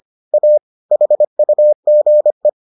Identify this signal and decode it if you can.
Morse code, heard as a keyed tone, about 25 wpm, 600 Hz. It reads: A HUGE